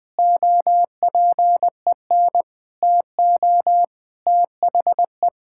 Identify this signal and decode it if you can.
Morse code, heard as a keyed tone, about 20 wpm, 700 Hz. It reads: OPEN TO THE
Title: Sentences with 7-Word Phrases - 20wpm